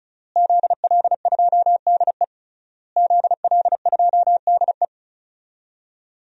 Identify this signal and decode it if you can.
Morse code, heard as a keyed tone, about 36 wpm, 700 Hz. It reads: ZL2DE ZL2DE
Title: DXCC-100 Call Signs - 5 Characters - Repeated 3x - 35wpm